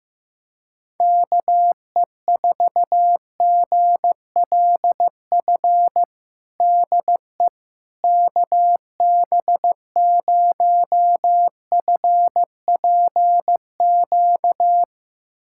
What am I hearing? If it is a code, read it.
Morse code, heard as a keyed tone, about 15 wpm, 700 Hz. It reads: KE4GLF DE KB0FPQ
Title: USA Callsign DE Callsign 15wpm